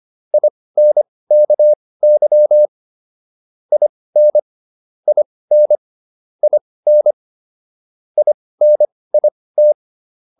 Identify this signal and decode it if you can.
Morse code, heard as a keyed tone, about 25 wpm, spaced as for 18 wpm, 600 Hz. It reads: INKY IN IN IN INIT